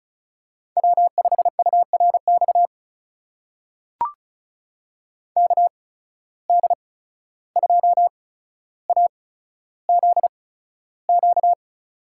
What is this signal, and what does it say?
Morse code, heard as a keyed tone, about 36 wpm, spaced as for 14 wpm, 700 Hz. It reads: W5URX KD2AZQ
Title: USA Call Signs - 8x Character Spacing - 35wpm